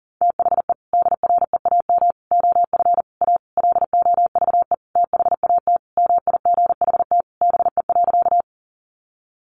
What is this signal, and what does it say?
Morse code, heard as a keyed tone, about 40 wpm, 700 Hz. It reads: THE DREAM OF A LOVE THAT MIGHT BE.
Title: Sentences from Top 1500 Words 40wpm